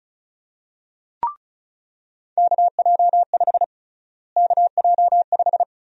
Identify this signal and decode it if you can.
Morse code, heard as a keyed tone, about 36 wpm, 700 Hz. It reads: KJ5 KJ5
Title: USA Call Sign Prefixes - Repeated 2x - 35wpm